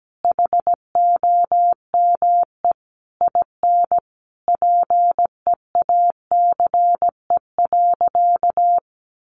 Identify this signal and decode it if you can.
Morse code, heard as a keyed tone, about 17 wpm, 700 Hz. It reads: HOME IN PEACE.
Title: Sentences from Top 1500 Words 17wpm